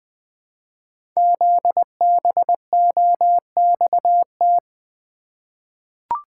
Morse code, heard as a keyed tone, about 20 wpm, 700 Hz. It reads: ZBOXT
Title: Code Groups - 5 Letters - 20wpm